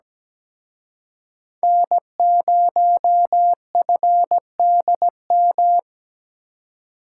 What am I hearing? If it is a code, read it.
Morse code, heard as a keyed tone, about 20 wpm, 700 Hz. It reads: N0FDM
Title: Speed Racing - USA 1x3 Call Signs 17wpm